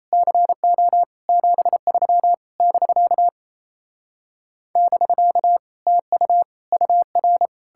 Morse code, heard as a keyed tone, about 38 wpm, 700 Hz. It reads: CO 73 <BK> <BK> TU UR